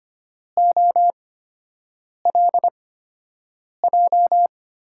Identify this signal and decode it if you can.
Morse code, heard as a keyed tone, about 25 wpm, spaced as for 7 wpm, 700 Hz. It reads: OLJ
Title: USA Call Signs - 6 Characters - 8x Character Spacing - 25wpm